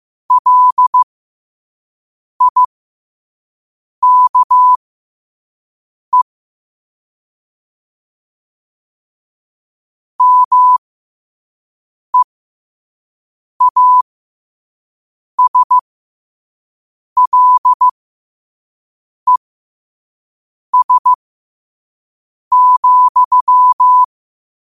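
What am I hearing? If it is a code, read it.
Morse code, heard as a keyed tone, about 15 wpm, spaced as for 5 wpm, 1000 Hz. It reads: LIKE MEASLES,
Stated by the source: 5 wpm Morse code practice recording.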